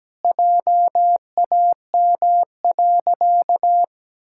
Morse code, heard as a keyed tone, about 17 wpm, 700 Hz. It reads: JAM.
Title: Pangram Challenge v2 - 17wpm